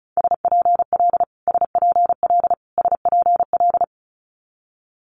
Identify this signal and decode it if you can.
Morse code, heard as a keyed tone, about 36 wpm, 700 Hz. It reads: SPL SPL SPL
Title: Multi-letter Phonemes Repeated 3x 35wpm